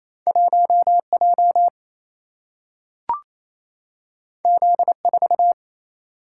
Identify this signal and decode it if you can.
Morse code, heard as a keyed tone, about 28 wpm, 700 Hz. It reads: 1 J Z 4
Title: Single-Letter Single-Number 28wpm